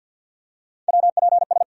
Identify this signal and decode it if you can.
Morse code, heard as a keyed tone, about 53 wpm, 700 Hz. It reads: WPS
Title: USA Call Sign Postfixes - 3 Letters 50wpm